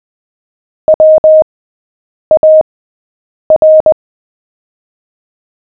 Morse code, heard as a keyed tone, about 20 wpm, spaced as for 8 wpm, 620 Hz. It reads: WAR
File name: cw / fists / Short-Story-7-8wpm